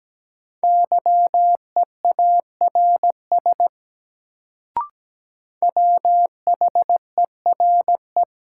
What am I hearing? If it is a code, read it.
Morse code, heard as a keyed tone, about 17 wpm, 700 Hz. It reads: YEARS WHERE
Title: Sentences from Top 200 Words - New Word Drill - 17wpm